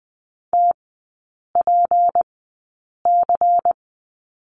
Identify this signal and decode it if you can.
Morse code, heard as a keyed tone, about 20 wpm, spaced as for 8 wpm, 690 Hz. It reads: TPC
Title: 212 Lesson 34e - Number 8 - Three Characters 20wpm